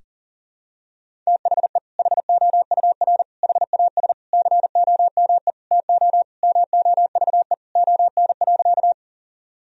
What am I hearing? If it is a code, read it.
Morse code, heard as a keyed tone, about 37 wpm, 700 Hz. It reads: THE HOUR HAS COME TO MOVE ON.
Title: Sentences from Top 1000 Words - Review Part 1 - 40wpm